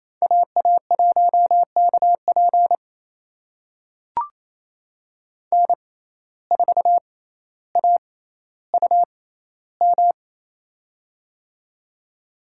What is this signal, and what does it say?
Morse code, heard as a keyed tone, about 28 wpm, spaced as for 14 wpm, 700 Hz. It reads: AA1KP N4AUM
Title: USA Call Signs - 5 Characters - 6x Character Spacing - 28wpm